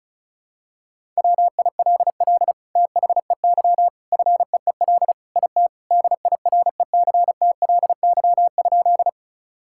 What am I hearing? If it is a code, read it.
Morse code, heard as a keyed tone, about 34 wpm, 700 Hz. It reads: WILL THEY FEEL IT DIRECTLY?
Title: Sentences from Top 1000 Words - Review Part 5 - 35wpm